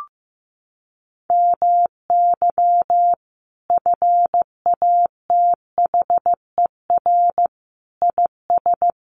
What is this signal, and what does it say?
Morse code, heard as a keyed tone, about 15 wpm, 700 Hz. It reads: MY FATHER IS
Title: Sentences from Top 300 Words 15wpm